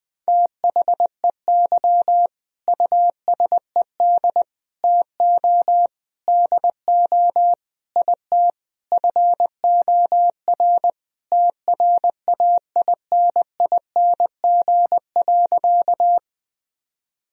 Morse code, heard as a keyed tone, about 20 wpm, 700 Hz. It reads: THEY USED TO DO IT FOR TRAINING.
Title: Sentences from Top 1000 Words - Review Part 4 - 20wpm